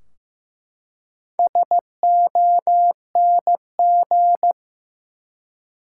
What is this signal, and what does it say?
Morse code, heard as a keyed tone, about 15 wpm, 700 Hz. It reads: SONG
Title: Four Letter Words - 15wpm